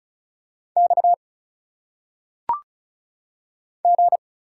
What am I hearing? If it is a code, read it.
Morse code, heard as a keyed tone, about 34 wpm, 700 Hz. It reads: X G
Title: Single Letters - Phonetic Pronunciation 35wpm